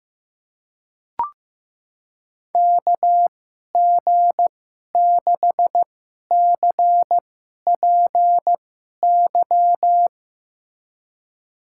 Morse code, heard as a keyed tone, about 15 wpm, spaced as for 11 wpm, 700 Hz. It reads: KG6CPY
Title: USA Call Signs - 6 Characters - 2x Character Spacing - 15wpm